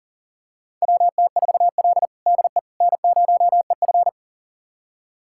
Morse code, heard as a keyed tone, about 40 wpm, 700 Hz. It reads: WT4L DE N0EF
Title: USA Callsign DE Callsign - 4 Characters - 40wpm